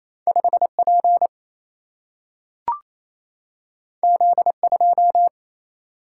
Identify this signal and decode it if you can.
Morse code, heard as a keyed tone, about 28 wpm, 700 Hz. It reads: 5 P Z 2
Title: Single-Letter Single-Number 28wpm